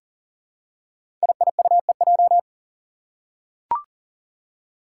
Morse code, heard as a keyed tone, about 39 wpm, 700 Hz. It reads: IIUEJ
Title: Five Challenging Letter-Number Combinations 40wpm